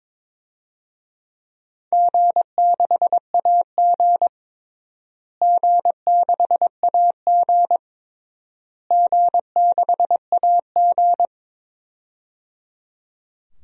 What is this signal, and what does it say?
Morse code, heard as a keyed tone, about 22 wpm, 700 Hz. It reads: G6AG G6AG G6AG
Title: DXCC-100 Call Signs - 4 Characters - Repeated 3x - 22wpm